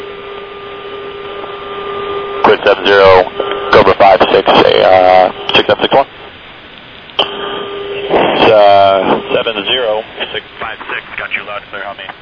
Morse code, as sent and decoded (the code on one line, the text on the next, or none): none